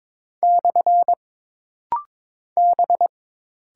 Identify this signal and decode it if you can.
Morse code, heard as a keyed tone, about 22 wpm, 700 Hz. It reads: / B
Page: This practice set is composed of Letters, Numbers, common punctuations, and common prosigns.